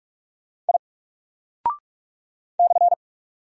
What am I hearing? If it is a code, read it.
Morse code, heard as a keyed tone, about 46 wpm, 700 Hz. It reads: I /
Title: Single Character (Letter:Number:Punctuation:Prosign) Alt 45wpm